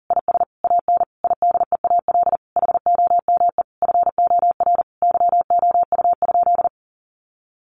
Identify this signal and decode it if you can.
Morse code, heard as a keyed tone, about 40 wpm, 700 Hz. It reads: IS AN IDEAL HOME FOR YOU?
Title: Sentences from Top 1400 Words 40wpm